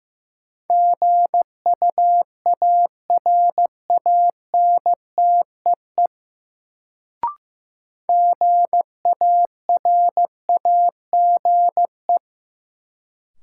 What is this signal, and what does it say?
Morse code, heard as a keyed tone, about 15 wpm, 700 Hz. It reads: GUARANTEE GARAGE
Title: Sentences from Top 1600 Words - New Word Drill - 15wpm